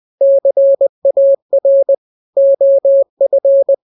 Morse code, heard as a keyed tone, about 20 wpm, 550 Hz. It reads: CAR OF
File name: jokes / joke 13 20 0000